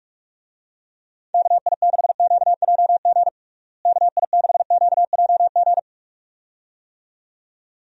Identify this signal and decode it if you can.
Morse code, heard as a keyed tone, about 46 wpm, 700 Hz. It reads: KI6QJG KI6QJG
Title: USA Call Signs - 6 Characters - Repeated 2x - 45wpm